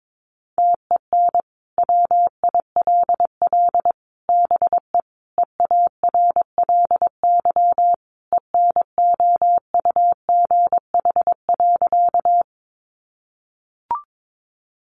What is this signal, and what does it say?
Morse code, heard as a keyed tone, about 22 wpm, 700 Hz. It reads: TEN WILL BE EARLY ENOUGH.
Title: Sentences from Top 400 Words 22wpm